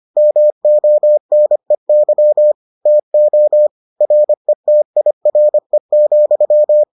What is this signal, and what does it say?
Morse code, heard as a keyed tone, about 25 wpm, 600 Hz. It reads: MONEY TO RETIRE,